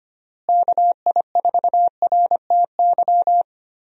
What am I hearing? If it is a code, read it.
Morse code, heard as a keyed tone, about 25 wpm, 700 Hz. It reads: KI4RTY
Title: USA Call Signs - 6 Characters - Repeated 2x - 25wpm